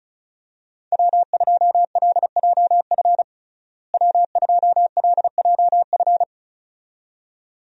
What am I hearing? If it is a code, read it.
Morse code, heard as a keyed tone, about 36 wpm, 700 Hz. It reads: W2LJF W2LJF
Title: USA Call Signs - 5 Characters - Repeated 2x - 35wpm